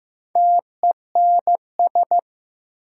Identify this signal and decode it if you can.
Morse code, heard as a keyed tone, about 15 wpm, 700 Hz. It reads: TENS
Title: Sets of 4 Words - New Word Drill - 15wpm